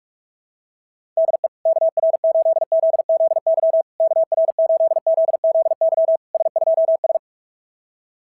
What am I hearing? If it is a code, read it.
Morse code, heard as a keyed tone, about 44 wpm, 660 Hz. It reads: DE KR8ZZY KR8ZZY S2S
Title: SOTA QSO Practice 45wpm